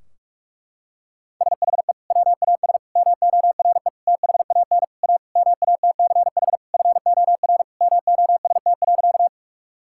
Morse code, heard as a keyed tone, about 46 wpm, 700 Hz. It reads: SHE WAS MORE THAN A MATCH FOR MOST.
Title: Sentences from Top 1600 Words 45wpm